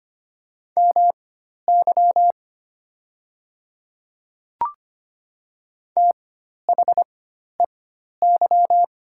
Morse code, heard as a keyed tone, about 25 wpm, spaced as for 12 wpm, 700 Hz. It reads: MY THEY
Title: 4x Character Spacing - Top 100 Words - 25wpm